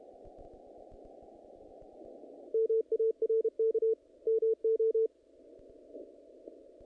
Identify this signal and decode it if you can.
Morse code, heard as a keyed tone, about 31 wpm, 450 Hz. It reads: MARK MO
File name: contests / naqp-cw-k5tr-op-k5pi- -2000